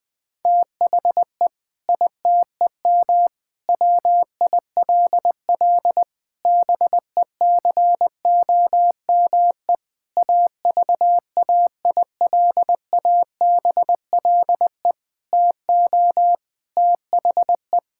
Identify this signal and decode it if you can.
Morse code, heard as a keyed tone, about 20 wpm, 700 Hz. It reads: THE ITEM WILL BECOME AVAILABLE TO THE
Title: Sentences from Top 1000 Words - Review Part 6 - 20wpm